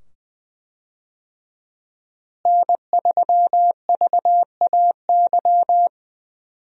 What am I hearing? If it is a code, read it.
Morse code, heard as a keyed tone, about 20 wpm, 700 Hz. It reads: N3VAY
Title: USA Call Signs - 5 Characters - Repeated 2x - 20wpm